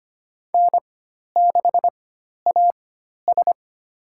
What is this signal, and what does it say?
Morse code, heard as a keyed tone, about 25 wpm, spaced as for 12 wpm, 700 Hz. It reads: N6AS
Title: USA Call Signs - 4 Characters - 4x Character Spacing - 25wpm